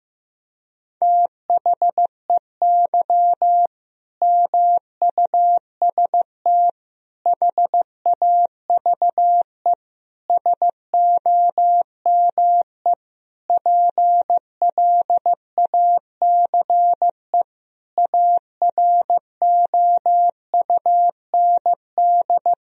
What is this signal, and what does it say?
Morse code, heard as a keyed tone, about 15 wpm, 700 Hz. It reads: THEY MUST HAVE SOME PLACE AROUND
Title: Sentences from Top 1000 Words - Review Part 3 - 15wpm